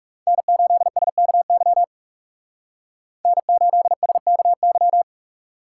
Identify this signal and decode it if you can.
Morse code, heard as a keyed tone, about 38 wpm, 690 Hz. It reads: N8SKY N8SKY